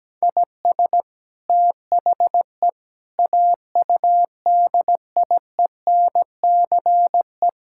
Morse code, heard as a keyed tone, about 17 wpm, 700 Hz. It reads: IS THE AUDIENCE
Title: Sentences from Top 1000 Words - Review Part 5 - 17wpm